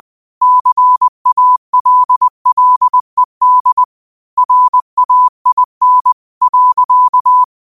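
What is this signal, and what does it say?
Morse code, heard as a keyed tone, about 20 wpm, 1000 Hz. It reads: CALLED RAIN.
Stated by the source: Morse code practice at 20 wpm: famous quotations sent in telegraphy, from URVAG.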